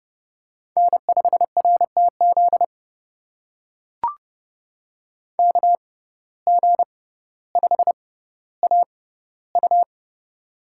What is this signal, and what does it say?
Morse code, heard as a keyed tone, about 30 wpm, spaced as for 15 wpm, 700 Hz. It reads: N5RTZ KG5AU
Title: USA Call Signs - 5 Characters - 6x Character Spacing - 30wpm